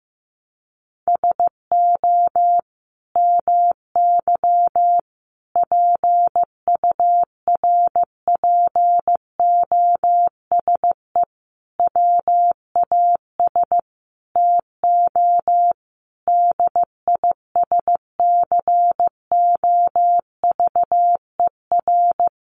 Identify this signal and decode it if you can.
Morse code, heard as a keyed tone, about 15 wpm, 700 Hz. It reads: SO MY PURPOSE WAS TO DISCOVER
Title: Sentences from Top 1100 Words 15wpm